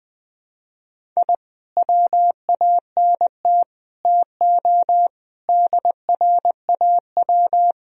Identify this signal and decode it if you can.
Morse code, heard as a keyed tone, about 20 wpm, 700 Hz. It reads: I WANT TO DRAW
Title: Sentences from Top 1000 Words - Review Part 1 - 20wpm